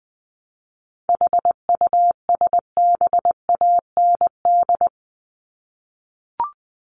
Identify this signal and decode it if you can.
Morse code, heard as a keyed tone, about 20 wpm, 700 Hz. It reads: HUSBAND
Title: Top 1501-1750 Words 20wpm